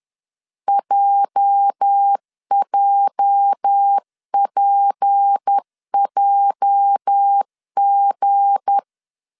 Morse code, heard as a keyed tone, about 10 wpm, 800 Hz. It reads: JJPJG
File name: LEZIONI / 8 YJ45 / 8 YJ 10wpm 2651